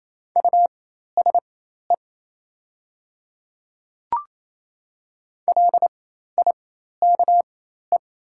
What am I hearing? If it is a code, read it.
Morse code, heard as a keyed tone, about 28 wpm, spaced as for 13 wpm, 700 Hz. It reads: USE LIKE
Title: 4x Character Spacing - Top 100 Words - 28wpm